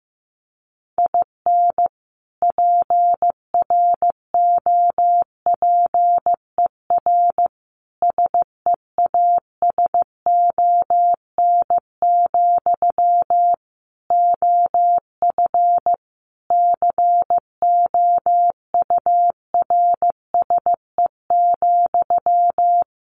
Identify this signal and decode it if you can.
Morse code, heard as a keyed tone, about 15 wpm, 700 Hz. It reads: IN PROPER SEASON, OF COURSE,
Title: Sentences from Top 1400 Words 15wpm